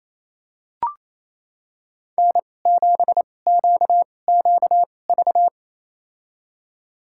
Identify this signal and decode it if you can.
Morse code, heard as a keyed tone, about 28 wpm, spaced as for 20 wpm, 700 Hz. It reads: N7QQV
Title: USA Call Signs - 5 Characters - 2x Character Spacing - 28wpm